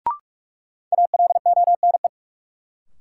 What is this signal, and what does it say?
Morse code, heard as a keyed tone, about 44 wpm, 700 Hz. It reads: ALONE
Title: Sentences from Top 1000 Words - New Word Drill - 45wpm